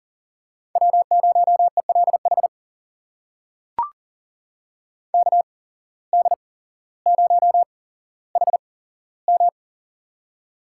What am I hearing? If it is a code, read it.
Morse code, heard as a keyed tone, about 38 wpm, spaced as for 17 wpm, 700 Hz. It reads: W0ELH KD0HM
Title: USA Call Signs - 5 Characters - 8x Character Spacing - 40wpm